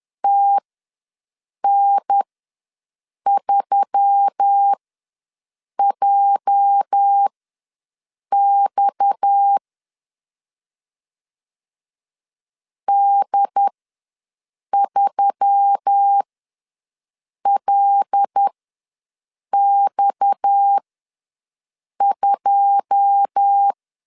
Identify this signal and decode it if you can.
Morse code, heard as a keyed tone, about 10 wpm, spaced as for 6 wpm, 800 Hz. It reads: TN3JX D3LX2
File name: LEZIONI / 9 286 ws 10wpm 4032